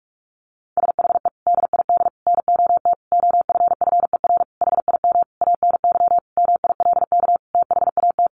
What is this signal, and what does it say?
Morse code, heard as a keyed tone, about 44 wpm, 700 Hz. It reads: SHE DID NOT OFFER HIM ANY MILK THAT
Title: Sentences from Top 1500 Words 45wpm